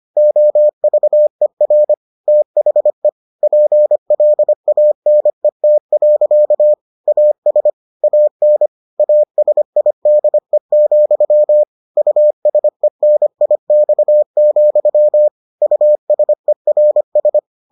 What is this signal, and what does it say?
Morse code, heard as a keyed tone, about 25 wpm, 600 Hz. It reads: OVER THE PLANET. AS AN ASIDE, USENIX, USERS